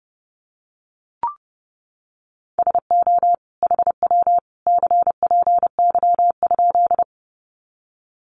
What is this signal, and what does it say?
Morse code, heard as a keyed tone, about 30 wpm, 700 Hz. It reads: SO HW CPY?
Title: Common QSO Phrases 30wpm